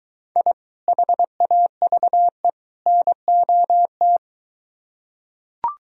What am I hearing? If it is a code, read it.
Morse code, heard as a keyed tone, about 21 wpm, 700 Hz. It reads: I HAVE NOT